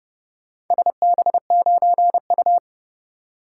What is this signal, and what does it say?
Morse code, heard as a keyed tone, about 30 wpm, 700 Hz. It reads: SB9U